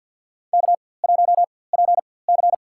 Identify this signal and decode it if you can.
Morse code, heard as a keyed tone, about 53 wpm, spaced as for 24 wpm, 700 Hz. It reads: K1PC
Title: USA Call Signs - 4x Character Spacing - 50wpm